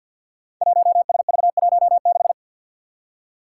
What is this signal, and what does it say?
Morse code, heard as a keyed tone, about 47 wpm, 700 Hz. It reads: 1SV16
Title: Five Challenging Letter-Number Combinations 50wpm